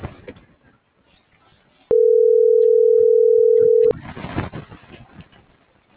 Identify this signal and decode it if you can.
Ringback tone (North America), 440 + 480 Hz.
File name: CENTRAL MAYAB